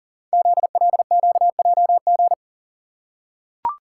Morse code, heard as a keyed tone, about 39 wpm, 690 Hz. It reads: ZLQJG
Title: Code Groups - 5 Letters - 40wpm